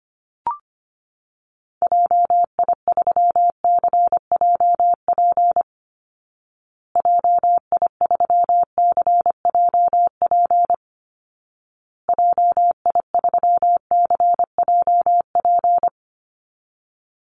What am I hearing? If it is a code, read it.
Morse code, heard as a keyed tone, about 25 wpm, 700 Hz. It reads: JI3CJP JI3CJP JI3CJP
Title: Call Signs Repeated 2x - 25wpm